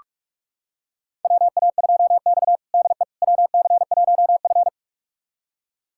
Morse code, heard as a keyed tone, about 47 wpm, 700 Hz. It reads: WA2X DE WC1F
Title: USA Callsign DE Callsign - 4 Characters - 45wpm